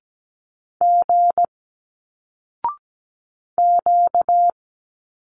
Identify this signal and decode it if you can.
Morse code, heard as a keyed tone, about 17 wpm, 700 Hz. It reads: G Q
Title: Single Letters 17wpm